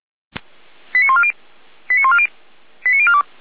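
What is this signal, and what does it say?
Five-tone selective call (ZVEI), heard as quick runs of five tones, tones of about 70 ms. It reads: 89140 89140 89042
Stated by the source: ZVEI five-tone call 89173 received by SDR, from feuerwehr-orlamuende.org (fire brigade Orlamünde).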